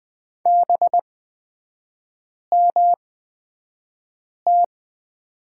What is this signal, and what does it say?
Morse code, heard as a keyed tone, about 20 wpm, spaced as for 5 wpm, 690 Hz. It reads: BMT
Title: 229 Lesson 37e - Letter x - Three Characters 20wpm